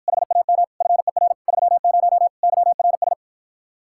Morse code, heard as a keyed tone, about 53 wpm, 700 Hz. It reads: HAM FER 30 YRS